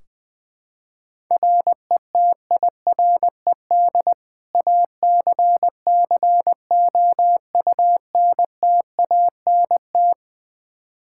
Morse code, heard as a keyed tone, about 20 wpm, 700 Hz. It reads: RETIRED ACCOUNTANT
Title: Practice QSO - Exchange 3 - More Info - 20wpm